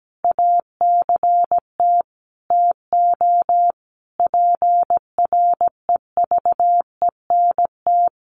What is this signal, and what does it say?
Morse code, heard as a keyed tone, about 17 wpm, 700 Hz. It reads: ACT TO PREVENT
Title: Sentences from Top 1100 Words 17wpm